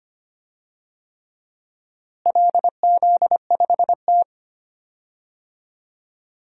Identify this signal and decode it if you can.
Morse code, heard as a keyed tone, about 25 wpm, 700 Hz. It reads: LZ5T
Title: DXCC-100 Call Signs - 4 Characters - Repeated 3x - 25wpm